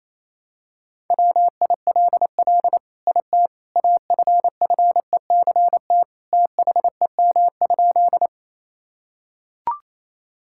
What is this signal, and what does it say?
Morse code, heard as a keyed tone, about 28 wpm, 700 Hz. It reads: WILL IT AFFECT THEM?
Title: Sentences from Top 1000 Words - Review Part 6 - 28wpm